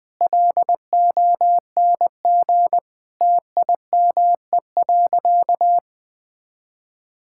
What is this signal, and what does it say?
Morse code, heard as a keyed tone, about 20 wpm, 700 Hz. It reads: LONG TIME.
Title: Sentences with 3-Word Phrases - 20wpm